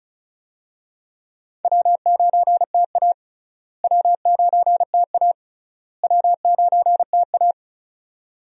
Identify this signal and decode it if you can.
Morse code, heard as a keyed tone, about 36 wpm, 700 Hz. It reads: W9TA W9TA W9TA
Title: USA Call Signs - 4 Characters - Repeated 3x - 35wpm